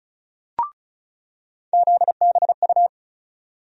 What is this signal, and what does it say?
Morse code, heard as a keyed tone, about 36 wpm, 700 Hz. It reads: ZBU
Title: USA Call Sign Postfixes - 3 Letters 35wpm